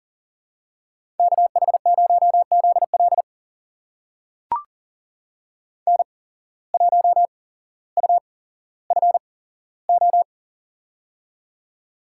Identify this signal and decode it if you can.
Morse code, heard as a keyed tone, about 38 wpm, spaced as for 17 wpm, 700 Hz. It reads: KH0ZL N1UFO
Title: USA Call Signs - 8x Character Spacing - 40wpm